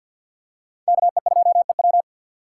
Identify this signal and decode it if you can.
Morse code, heard as a keyed tone, about 51 wpm, 700 Hz. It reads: KE2EW
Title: USA Call Signs - 5 Characters - 2x Character Spacing - 50wpm